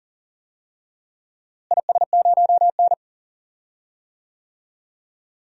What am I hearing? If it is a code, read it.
Morse code, heard as a keyed tone, about 40 wpm, 700 Hz. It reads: IS0N
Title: DXCC-100 Call Signs - 4 Characters - Repeated 2x - 40wpm